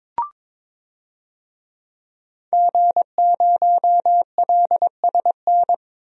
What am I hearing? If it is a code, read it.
Morse code, heard as a keyed tone, about 22 wpm, 700 Hz. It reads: G0LSN
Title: DXCC-100 Call Signs - 5 Characters - 22wpm